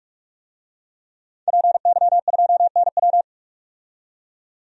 Morse code, heard as a keyed tone, about 42 wpm, 700 Hz. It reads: PY2NW
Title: DXCC-100 Call Signs - 5 Characters - 45wpm